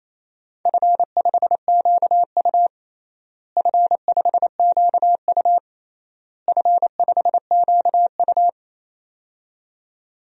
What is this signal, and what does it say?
Morse code, heard as a keyed tone, about 28 wpm, 700 Hz. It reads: F5QU F5QU F5QU
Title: DXCC-100 Call Signs - 4 Characters - Repeated 3x - 28wpm